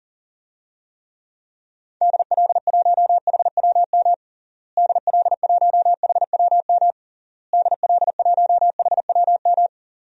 Morse code, heard as a keyed tone, about 40 wpm, 700 Hz. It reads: DL1HWM DL1HWM DL1HWM
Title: DXCC-100 Call Signs - 6 Characters - Repeated 3x - 40wpm